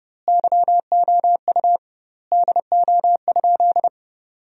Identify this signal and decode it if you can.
Morse code, heard as a keyed tone, about 32 wpm, 700 Hz. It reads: YOU DO?
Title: 2x Word-Spacing - Sentences from Top 100 Words - 30wpm